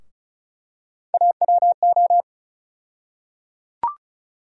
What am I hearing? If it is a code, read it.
Morse code, heard as a keyed tone, about 36 wpm, 700 Hz. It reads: AWO
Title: USA Call Sign Postfixes - 3 Letters 35wpm